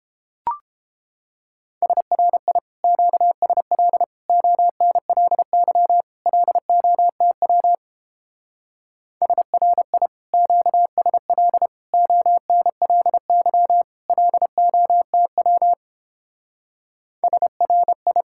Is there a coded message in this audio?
Yes: Morse code from 1.8 to 18.2 s, heard as a keyed tone, about 29 wpm, 700 Hz. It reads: SRI QSL ONLY LOTW SRI QSL ONLY LOTW SRI